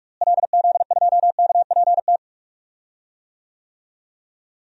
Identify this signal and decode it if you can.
Morse code, heard as a keyed tone, about 46 wpm, 700 Hz. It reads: LZ2KPT
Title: DXCC-100 Call Signs - 6 Characters - Repeated 3x - 45wpm